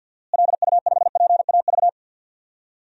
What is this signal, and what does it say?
Morse code, heard as a keyed tone, about 51 wpm, 700 Hz. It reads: LU5PAV
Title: DXCC-100 Call Signs - 6 Characters - Repeated 2x - 50wpm